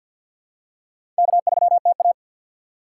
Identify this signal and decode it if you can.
Morse code, heard as a keyed tone, about 49 wpm, 700 Hz. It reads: K3TA
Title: USA Call Signs - 4x Character Spacing - 50wpm